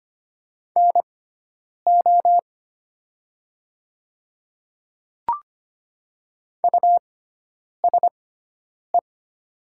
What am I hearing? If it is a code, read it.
Morse code, heard as a keyed tone, about 25 wpm, spaced as for 9 wpm, 700 Hz. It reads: NO USE
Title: 6x Character Spacing - Top 100 Words - 25wpm